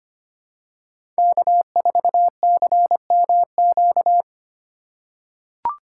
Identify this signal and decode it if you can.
Morse code, heard as a keyed tone, about 25 wpm, 700 Hz. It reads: K4CMQ
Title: USA Call Signs - 5 Characters - 2x Character Spacing - 25wpm